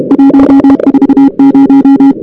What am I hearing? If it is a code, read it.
Morse code, heard as a keyed tone, about 31 wpm, 280 Hz. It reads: 140
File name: CWSprint / Feb2006 / 2006-feb-cw-sprint-k5tr op-n5rz -02:00